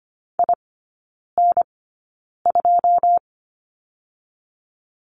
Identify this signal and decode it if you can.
Morse code, heard as a keyed tone, about 25 wpm, spaced as for 9 wpm, 700 Hz. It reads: IN2
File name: Lesson 15e - Number 5 - Three Characters 25wpm